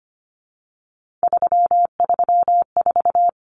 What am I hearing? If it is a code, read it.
Morse code, heard as a keyed tone, about 25 wpm, 700 Hz. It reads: 334
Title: Numbers - 3-Digit 25wpm